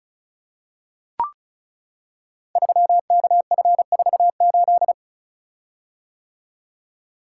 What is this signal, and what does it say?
Morse code, heard as a keyed tone, about 36 wpm, 700 Hz. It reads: 3KF48
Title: Code Groups - 5 Letter:Numbers - 35wpm